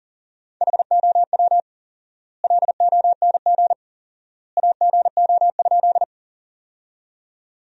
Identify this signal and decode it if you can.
Morse code, heard as a keyed tone, about 40 wpm, 700 Hz. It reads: HOW LONG AGO?
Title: 4x Word-Spacing - Sentences from Top 500 Words - 40wpm